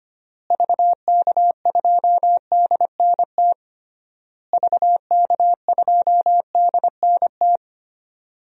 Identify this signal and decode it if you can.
Morse code, heard as a keyed tone, about 25 wpm, 700 Hz. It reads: VK2DNT VK2DNT
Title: DXCC-100 Call Signs - 6 Characters - Repeated 3x - 25wpm